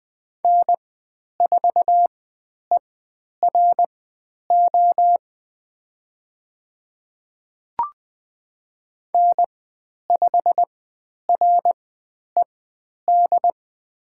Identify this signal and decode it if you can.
Morse code, heard as a keyed tone, about 20 wpm, spaced as for 10 wpm, 700 Hz. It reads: N4ERO N5RED
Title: 102 Lesson 17g - Letter u - Call Signs 20wpm